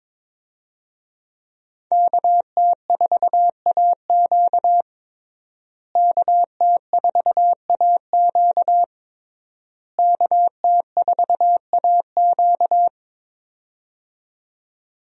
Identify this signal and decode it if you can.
Morse code, heard as a keyed tone, about 22 wpm, 700 Hz. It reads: KT4AQ KT4AQ KT4AQ
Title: USA Call Signs - 5 Characters - Repeated 3x - 22wpm